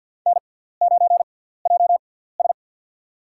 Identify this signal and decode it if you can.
Morse code, heard as a keyed tone, about 53 wpm, spaced as for 17 wpm, 690 Hz. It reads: N9JS
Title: USA Call Signs - 4 Characters - 6x Character Spacing - 50wpm